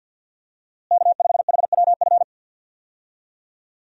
Morse code, heard as a keyed tone, about 47 wpm, 700 Hz. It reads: K5HWF
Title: USA Call Signs - 5 Characters - Repeated 3x - 50wpm